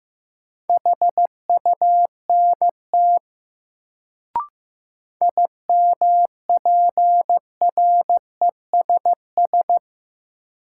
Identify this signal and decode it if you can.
Morse code, heard as a keyed tone, about 15 wpm, 700 Hz. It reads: HUNT IMPRESS
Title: Sentences from Top 1600 Words - New Word Drill - 15wpm